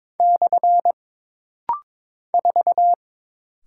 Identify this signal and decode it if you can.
Morse code, heard as a keyed tone, about 22 wpm, 700 Hz. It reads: / 4
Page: This rapid-fire timing is geared to push your ICR (Instant Character Recognition) skills.